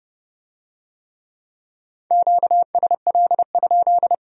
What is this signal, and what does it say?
Morse code, heard as a keyed tone, about 29 wpm, 700 Hz. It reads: QSL?